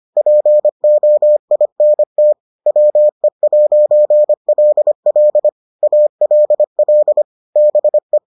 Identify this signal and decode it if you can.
Morse code, heard as a keyed tone, about 25 wpm, 600 Hz. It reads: POINT WE'LL ALL BE